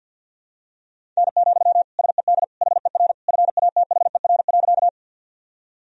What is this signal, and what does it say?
Morse code, heard as a keyed tone, about 51 wpm, 700 Hz. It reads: N, SED HER FATHER.
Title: Sentences from Top 300 Words and Abbreviations 50wpm